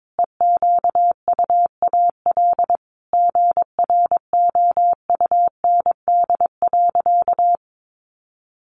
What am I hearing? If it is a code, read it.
Morse code, heard as a keyed tone, about 22 wpm, 700 Hz. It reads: EQUAL GROUND.
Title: Sentences from Top 1400 Words 22wpm